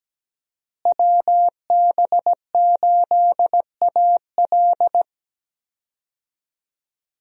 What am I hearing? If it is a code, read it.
Morse code, heard as a keyed tone, about 17 wpm, 700 Hz. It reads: WB8AL
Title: USA Call Signs - 5 Characters - Repeated 2x - 17wpm